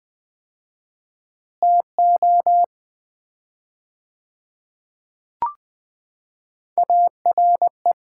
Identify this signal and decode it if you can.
Morse code, heard as a keyed tone, about 20 wpm, 700 Hz. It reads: TO ARE
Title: 4x Word-Spacing - Sets of 4 Words - 20wpm